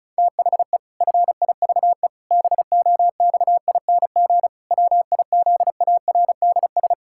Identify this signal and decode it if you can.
Morse code, heard as a keyed tone, about 34 wpm, 700 Hz. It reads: THE FIVE BOXING WIZARDS
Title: Pangram Challenge v2 - 35wpm